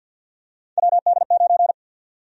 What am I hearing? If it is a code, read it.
Morse code, heard as a keyed tone, about 50 wpm, 700 Hz. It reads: WD9
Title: USA Call Sign Prefixes - 50wpm